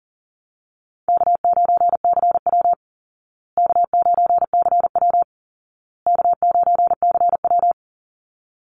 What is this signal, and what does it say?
Morse code, heard as a keyed tone, about 40 wpm, 700 Hz. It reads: K9CW K9CW K9CW
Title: Easy Call Signs Repeated 3x - 40wpm